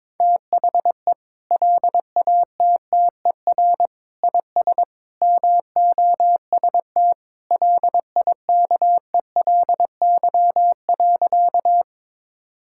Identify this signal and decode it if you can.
Morse code, heard as a keyed tone, about 22 wpm, 700 Hz. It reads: THE LATTER IS MOST LIKELY.
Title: Sentences from Top 1600 Words 22wpm